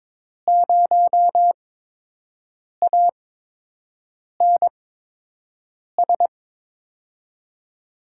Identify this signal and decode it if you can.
Morse code, heard as a keyed tone, about 22 wpm, spaced as for 6 wpm, 700 Hz. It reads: 0ANS
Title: USA Call Signs - 5 Characters - 8x Character Spacing - 22wpm